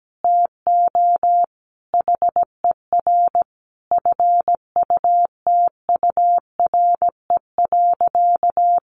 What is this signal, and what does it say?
Morse code, heard as a keyed tone, about 17 wpm, 700 Hz. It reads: TO HER FUTURE.
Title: Sentences from Top 1200 Words 17wpm